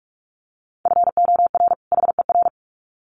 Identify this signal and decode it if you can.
Morse code, heard as a keyed tone, about 44 wpm, 700 Hz. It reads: FOR HER
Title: Sets of 2 Words 45wpm